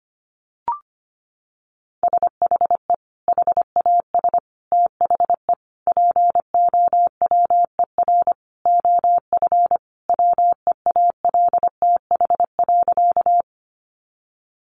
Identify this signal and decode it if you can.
Morse code, heard as a keyed tone, about 25 wpm, 700 Hz. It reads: SHE HAS THE POWER OF WEALTH.
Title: Sentences from Top 1200 Words 25wpm